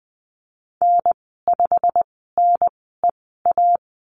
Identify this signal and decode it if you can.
Morse code, heard as a keyed tone, about 20 wpm, spaced as for 15 wpm, 700 Hz. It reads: N5NEA
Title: 088 Lesson 15g - Number 5 - Call Signs 20wpm